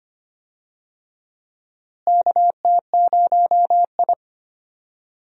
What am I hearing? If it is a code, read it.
Morse code, heard as a keyed tone, about 25 wpm, 700 Hz. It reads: KT0I
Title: USA Call Signs - 4 Characters - Repeated 2x - 25wpm